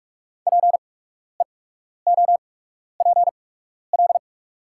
Morse code, heard as a keyed tone, about 43 wpm, spaced as for 12 wpm, 700 Hz. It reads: PEOPL
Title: 8x Character Spacing - Top 100 Words - 45wpm